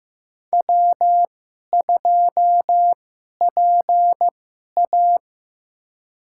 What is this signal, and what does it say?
Morse code, heard as a keyed tone, about 15 wpm, spaced as for 11 wpm, 700 Hz. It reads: W2PA
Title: USA Call Signs - 4 Characters - 2x Character Spacing - 15wpm